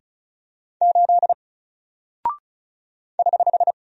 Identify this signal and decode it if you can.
Morse code, heard as a keyed tone, about 36 wpm, 700 Hz. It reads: 8 <HH>
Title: Single Character (Letter:Number:Punctuation:Prosign) 35wpm